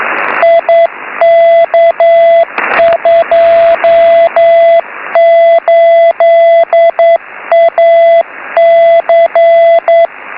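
Morse code, heard as a keyed tone, about 9 wpm, 670 Hz, with radio noise. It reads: IK28AC
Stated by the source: Beacon 5T5DUB heard on 6m by IW0FFK.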